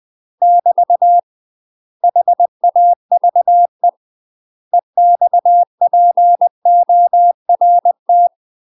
Morse code, heard as a keyed tone, about 20 wpm, 700 Hz. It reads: <BT> HAVE EXPORT
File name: UK MASTER QSOs TRAINING / 20WPM / UK MASTER QSOs 0034